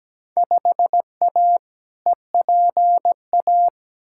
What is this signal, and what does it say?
Morse code, heard as a keyed tone, about 17 wpm, 700 Hz. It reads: 5A EPA
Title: Field Day Exchanges - 17wpm